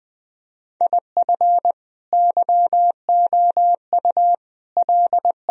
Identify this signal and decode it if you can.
Morse code, heard as a keyed tone, about 22 wpm, 700 Hz. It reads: IF YOU L